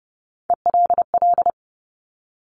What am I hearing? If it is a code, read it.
Morse code, heard as a keyed tone, about 30 wpm, 690 Hz. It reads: ELL
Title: Top Three Letter Combinations in Words 30wpm